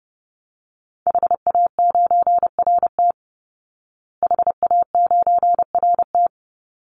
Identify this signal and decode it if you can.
Morse code, heard as a keyed tone, about 29 wpm, 700 Hz. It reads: HA9RT HA9RT
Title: Call Signs Repeated 3x - 30wpm